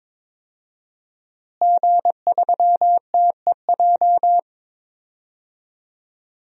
Morse code, heard as a keyed tone, about 22 wpm, 700 Hz. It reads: G3TEJ
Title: DXCC-100 Call Signs - 5 Characters - Repeated 3x - 22wpm